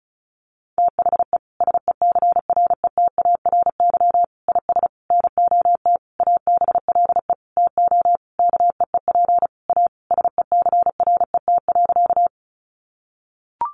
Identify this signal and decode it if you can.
Morse code, heard as a keyed tone, about 34 wpm, 700 Hz. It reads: THE SECRETARY IS NOT ABLE TO KEEP A SECRET.
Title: Sentences from Top 1300 Words 35wpm